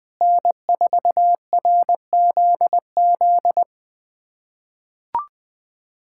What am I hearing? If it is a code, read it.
Morse code, heard as a keyed tone, about 20 wpm, 700 Hz. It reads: N4RZZ
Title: USA Callsign DE Callsign - 5 Characters - 20wpm